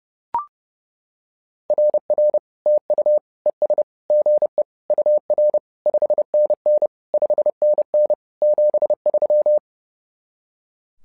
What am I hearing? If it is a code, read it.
Morse code, heard as a keyed tone, about 30 wpm, 600 Hz. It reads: RR TU ES GE UR 5NN 5NN 73
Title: SOTA QSO Practice 30wpm